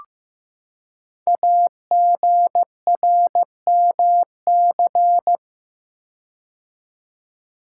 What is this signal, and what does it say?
Morse code, heard as a keyed tone, about 15 wpm, 700 Hz. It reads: AGRMC
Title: Code Groups - 5 Letters - 15wpm